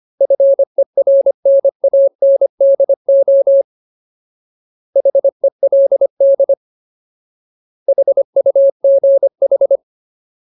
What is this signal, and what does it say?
Morse code, heard as a keyed tone, about 25 wpm, 550 Hz. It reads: FERNANDO HELD HUGH
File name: Ultimate Word List / 25WPM Ultimate